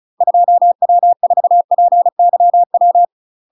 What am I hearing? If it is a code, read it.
Morse code, heard as a keyed tone, about 36 wpm, 700 Hz. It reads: 2W4PYW
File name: UK MASTER QSOs TRAINING / 35WPM / UK MASTER QSOs 0031